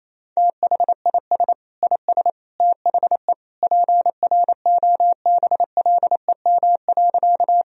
Morse code, heard as a keyed tone, about 28 wpm, 700 Hz. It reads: THIS IS THE PROBLEM.
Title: Sentences from Top 1000 Words - Review Part 2 - 28wpm